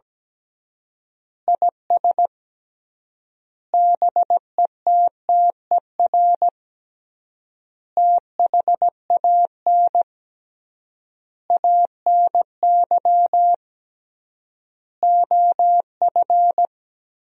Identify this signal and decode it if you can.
Morse code, heard as a keyed tone, about 17 wpm, 700 Hz. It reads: IS BETTER THAN ANY OF
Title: 3x Word-Spacing - Sentences from Top 400 Words - 17wpm